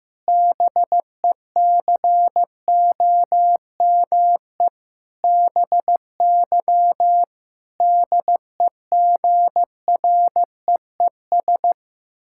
Morse code, heard as a keyed tone, about 15 wpm, 700 Hz. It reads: BECOME BY DEGREES
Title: Pangram Challenge v2 - 15wpm